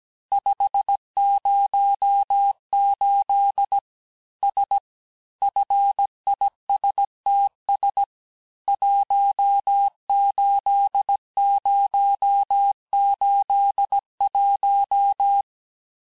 Morse code, heard as a keyed tone, about 17 wpm, 800 Hz. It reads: 508 S FISTS 18081
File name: cours-du-lundi files / cours-1587-FIN-de-QSO-vitesse-16MPM